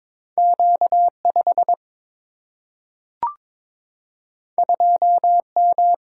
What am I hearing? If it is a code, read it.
Morse code, heard as a keyed tone, about 22 wpm, 700 Hz. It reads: Q5 2M
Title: Single-Letter Single-Number 22wpm